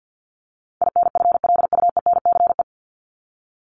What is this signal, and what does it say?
Morse code, heard as a keyed tone, about 50 wpm, 700 Hz. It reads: INFLUENCE
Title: Top 1001-1250 Words 50wpm